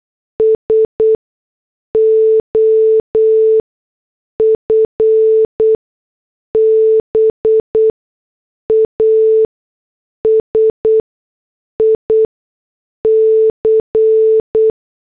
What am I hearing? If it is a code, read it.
Morse code, heard as a keyed tone, about 8 wpm, spaced as for 6 wpm, 440 Hz. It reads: S OF BASIC